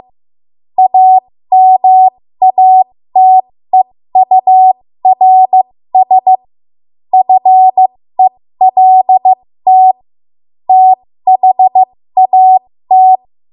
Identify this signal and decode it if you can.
Morse code, heard as a keyed tone, about 15 wpm, spaced as for 13 wpm, 750 Hz. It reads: AMATEURS FELT THAT
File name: Morse / 13WPM